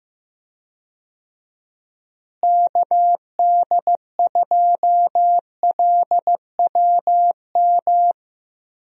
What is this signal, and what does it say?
Morse code, heard as a keyed tone, about 15 wpm, 700 Hz. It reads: KD2LWM
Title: USA Call Signs - 6 Characters - Repeated 3x - 15wpm